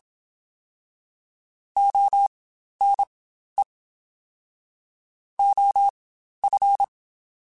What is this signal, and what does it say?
Morse code, heard as a keyed tone, about 27 wpm, spaced as for 12 wpm, 800 Hz. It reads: ONE OF
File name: Quotes Spaced / Quotes Spaced-mp3s / ChildAsleep sp @25